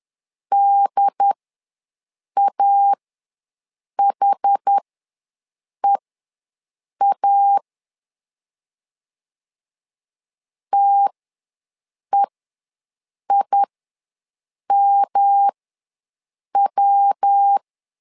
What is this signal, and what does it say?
Morse code, heard as a keyed tone, about 10 wpm, spaced as for 6 wpm, 800 Hz. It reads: DAHEA TEIMW
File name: LEZIONI / 3 MWRO / 3 MW ws 10wpm 2213